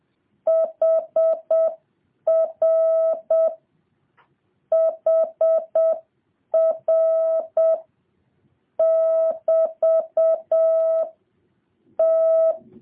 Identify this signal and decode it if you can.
Morse code, heard as a keyed tone, about 7 wpm, 650 Hz. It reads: HR HR <BT> T